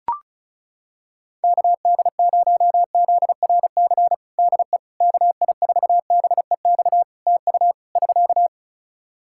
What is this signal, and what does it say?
Morse code, heard as a keyed tone, about 36 wpm, 700 Hz. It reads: KD0ZRC DE KI4BEX TU <SK>
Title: Practice QSO - Exchange 4 - Ending - 35wpm